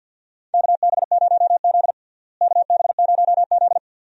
Morse code, heard as a keyed tone, about 49 wpm, 700 Hz. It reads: KB0Z KB0Z
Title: USA Call Signs - 4 Characters - Repeated 2x - 50wpm